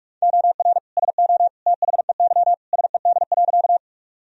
Morse code, heard as a keyed tone, about 46 wpm, 700 Hz. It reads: OR SO THEY SED.
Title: Sentences from Top 100 Words and Abbreviations 45wpm